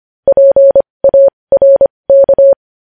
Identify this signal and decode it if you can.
Morse code, heard as a keyed tone, about 25 wpm, spaced as for 20 wpm, 550 Hz. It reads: PARK
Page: news headlines in morse code at 20 wpm